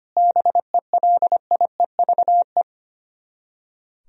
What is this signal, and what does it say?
Morse code, heard as a keyed tone, about 25 wpm, 700 Hz. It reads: BELIEVE
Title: Speed-Racing - Sets of 4 Words 25wpm